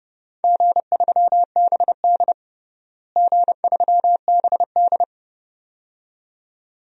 Morse code, heard as a keyed tone, about 30 wpm, 700 Hz. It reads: G3BD G3BD
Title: DXCC-100 Call Signs - 4 Characters - Repeated 2x - 30wpm